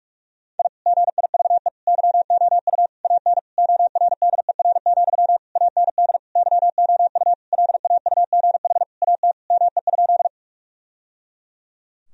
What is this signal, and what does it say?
Morse code, heard as a keyed tone, about 46 wpm, 700 Hz. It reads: I GIVE YOU AN ORDER, AND YOU LAUGH AT ME?
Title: Sentences from Top 1000 Words - Review Part 2 - 45wpm